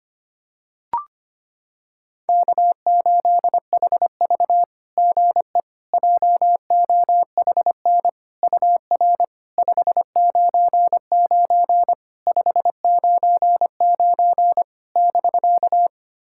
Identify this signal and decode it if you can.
Morse code, heard as a keyed tone, about 25 wpm, 690 Hz. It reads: K8HV GE JOHN UR 599 599 <BK>
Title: SOTA QSO Practice 25wpm